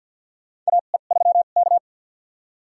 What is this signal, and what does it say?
Morse code, heard as a keyed tone, about 48 wpm, spaced as for 35 wpm, 700 Hz. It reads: AE3K
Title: USA Call Signs - 4 Characters - 2x Character Spacing - 50wpm